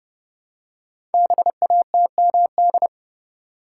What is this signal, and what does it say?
Morse code, heard as a keyed tone, about 29 wpm, 700 Hz. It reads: BATMD
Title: Code Groups - 5 Letters - 30wpm